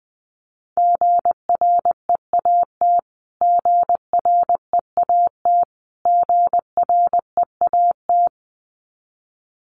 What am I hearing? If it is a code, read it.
Morse code, heard as a keyed tone, about 20 wpm, 700 Hz. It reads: GREAT GREAT GREAT
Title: Top 300 Words Repeated 20wpm